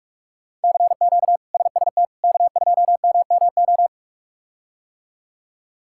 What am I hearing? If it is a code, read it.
Morse code, heard as a keyed tone, about 47 wpm, 700 Hz. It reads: CQ SST K2MMO
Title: K1USN SST QSO Practice - 45wpm